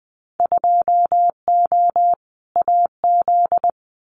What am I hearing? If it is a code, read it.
Morse code, heard as a keyed tone, about 20 wpm, 700 Hz. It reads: 2O AZ
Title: Winter Field Day Exchanges 20wpm